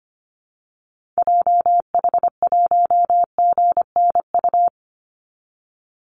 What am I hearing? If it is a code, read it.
Morse code, heard as a keyed tone, about 25 wpm, 700 Hz. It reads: JH1GNU
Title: Call Signs Repeated 2x - 25wpm